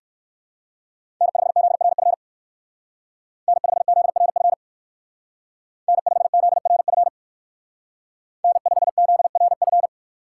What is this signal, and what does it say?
Morse code, heard as a keyed tone, about 56 wpm, 700 Hz. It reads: N5ZRF N5ZRF N5ZRF N5ZRF
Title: Speed Racing - USA 1x3 Call Signs 45wpm